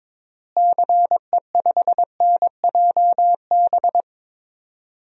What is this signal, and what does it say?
Morse code, heard as a keyed tone, about 22 wpm, 700 Hz. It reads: CE5NJB
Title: DXCC-100 Call Signs - 6 Characters - Repeated 3x - 22wpm